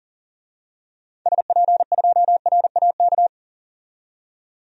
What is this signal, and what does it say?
Morse code, heard as a keyed tone, about 40 wpm, 700 Hz. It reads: SP2RAK
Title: DXCC-100 Call Signs - 6 Characters - 40wpm